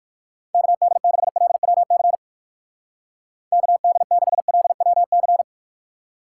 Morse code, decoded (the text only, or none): KD6LWC KD6LWC